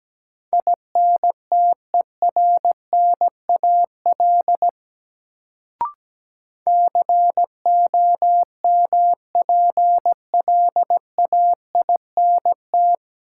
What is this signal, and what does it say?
Morse code, heard as a keyed tone, about 17 wpm, 700 Hz. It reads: INTERNAL COMPLAINT
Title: Sentences from Top 1400 Words - New Word Drill Alt - 17wpm